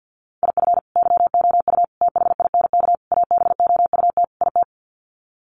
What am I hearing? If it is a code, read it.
Morse code, heard as a keyed tone, about 50 wpm, 700 Hz. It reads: IF YOU THINK ABOUT IT
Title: Sets of 5 Words Encore2 50wpm